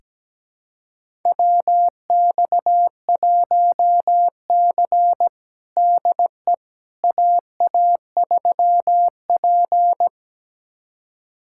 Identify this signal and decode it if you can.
Morse code, heard as a keyed tone, about 17 wpm, 700 Hz. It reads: WX1C DE AA3P
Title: USA Callsign DE Callsign - 4 Characters - 17wpm